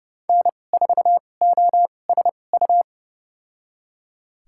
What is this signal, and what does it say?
Morse code, heard as a keyed tone, about 30 wpm, spaced as for 22 wpm, 690 Hz. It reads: N4OSU
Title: USA Call Signs - 5 Characters - 2x Character Spacing - 30wpm